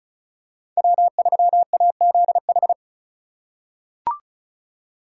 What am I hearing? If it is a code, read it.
Morse code, heard as a keyed tone, about 36 wpm, 700 Hz. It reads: W3AZH
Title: USA Call Signs - 2x Character Spacing - 35wpm